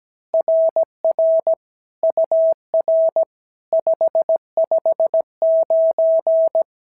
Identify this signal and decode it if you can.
Morse code, heard as a keyed tone, about 16 wpm, 650 Hz. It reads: RR UR 559
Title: SOTA QSO Practice 17wpm